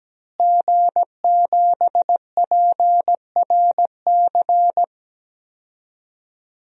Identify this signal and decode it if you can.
Morse code, heard as a keyed tone, about 17 wpm, 700 Hz. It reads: G7PRC
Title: DXCC-100 Call Signs - 5 Characters - Repeated 2x - 17wpm